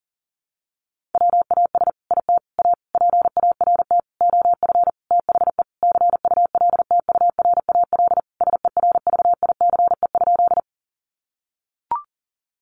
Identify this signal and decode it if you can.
Morse code, heard as a keyed tone, about 40 wpm, 700 Hz. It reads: WAS IT A PART OF THE CULTURAL SERVICE?
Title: Sentences from Top 1500 Words 40wpm